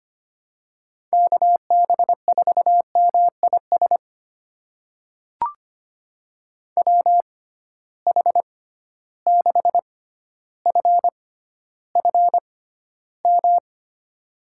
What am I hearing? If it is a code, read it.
Morse code, heard as a keyed tone, about 25 wpm, spaced as for 13 wpm, 700 Hz. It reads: KB4MIS WH6FFM
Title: USA Call Signs - 6 Characters - 6x Character Spacing - 25wpm